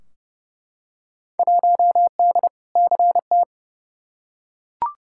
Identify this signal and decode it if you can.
Morse code, heard as a keyed tone, about 30 wpm, 700 Hz. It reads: 1D CT
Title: Field Day Exchanges - 30wpm